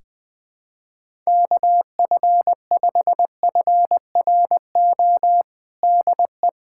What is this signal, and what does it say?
Morse code, heard as a keyed tone, about 20 wpm, 700 Hz. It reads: KF5FRO DE
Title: USA Callsign DE Callsign - 6 Characters - 20wpm